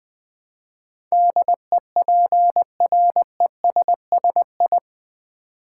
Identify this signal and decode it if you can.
Morse code, heard as a keyed tone, about 20 wpm, 700 Hz. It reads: DEPRESSI
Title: Build A Word - 4 Letters At A Time - 10-Letter Words - 20wpm